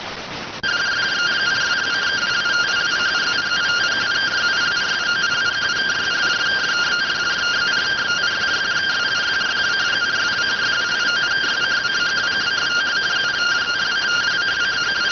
Two-tone teleprinter signal, about 170 Hz shift.